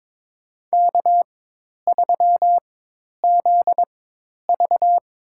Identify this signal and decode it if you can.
Morse code, heard as a keyed tone, about 22 wpm, spaced as for 10 wpm, 700 Hz. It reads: K3ZV
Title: USA Call Signs - 4 Characters - 4x Character Spacing - 22wpm